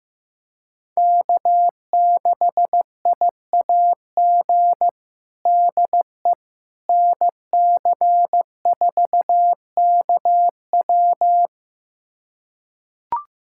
Morse code, heard as a keyed tone, about 15 wpm, 700 Hz. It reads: K6IAG DE NC4KW
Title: USA Callsign DE Callsign - 5 Characters - 15wpm